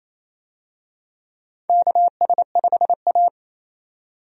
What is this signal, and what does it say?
Morse code, heard as a keyed tone, about 28 wpm, 700 Hz. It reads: KS5A
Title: USA Call Signs - 4 Characters - 28wpm